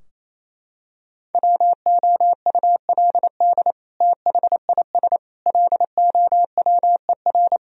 Morse code, heard as a keyed tone, about 28 wpm, 700 Hz. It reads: WOULD THIS LOWER
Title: Sentences from Top 1000 Words - Review Part 4 - 28wpm